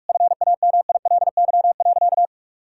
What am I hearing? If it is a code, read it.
Morse code, heard as a keyed tone, about 44 wpm, 700 Hz. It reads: FAMILY.
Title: Sentences from Top 1600 Words 45wpm